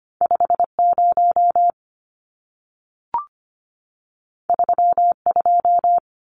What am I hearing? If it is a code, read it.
Morse code, heard as a keyed tone, about 25 wpm, 700 Hz. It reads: 50 32
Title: Numbers - 2-Digit 25wpm